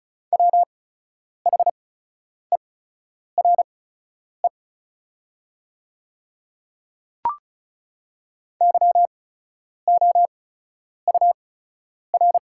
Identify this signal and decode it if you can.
Morse code, heard as a keyed tone, about 36 wpm, spaced as for 10 wpm, 690 Hz. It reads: WHERE YOUR
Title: 8x Character Spacing - Top 500 Words - 35wpm